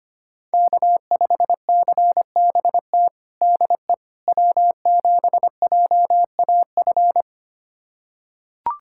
Morse code, heard as a keyed tone, about 25 wpm, 700 Hz. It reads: K5CBT DE W7JAF
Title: USA Callsign DE Callsign - 5 Characters - 25wpm